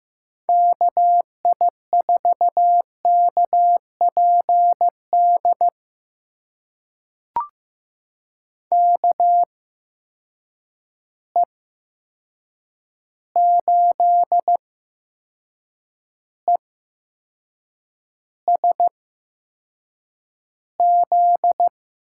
Morse code, heard as a keyed tone, about 15 wpm, spaced as for 6 wpm, 700 Hz. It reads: KI4KPD KE8ESZ
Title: USA Call Signs - 6 Characters - 8x Character Spacing - 15wpm